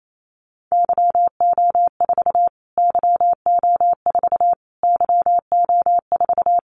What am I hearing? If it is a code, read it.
Morse code, heard as a keyed tone, about 28 wpm, 700 Hz. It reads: YO4 YO4 YO4
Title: Callsign Prefixes for Top 25 DXCC - Repeated 3x - 28wpm